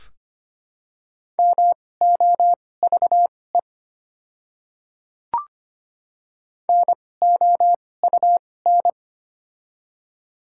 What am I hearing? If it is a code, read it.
Morse code, heard as a keyed tone, about 25 wpm, spaced as for 18 wpm, 700 Hz. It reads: MOVE NOUN
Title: 2x Character Spacing - Top 500 Words - 25wpm